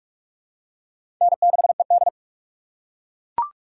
Morse code, heard as a keyed tone, about 41 wpm, 700 Hz. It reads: N6ED
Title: USA Call Signs - 4 Characters - 3x Character Spacing - 45wpm